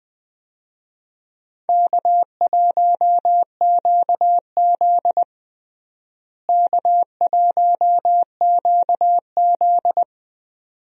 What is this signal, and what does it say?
Morse code, heard as a keyed tone, about 20 wpm, 700 Hz. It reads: K1QZ K1QZ
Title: USA Call Signs - 4 Characters - Repeated 2x - 20wpm